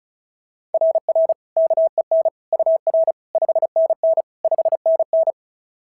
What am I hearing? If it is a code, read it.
Morse code, heard as a keyed tone, about 33 wpm, 660 Hz. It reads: RR KEN UR 5NN 5NN
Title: SOTA QSO Practice 35wpm